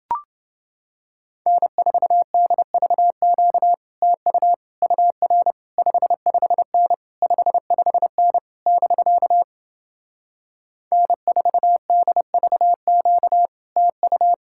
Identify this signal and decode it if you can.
Morse code, heard as a keyed tone, about 32 wpm, 700 Hz. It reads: N4DVQ TU UR 55N 55N <BK> N4DVQ TU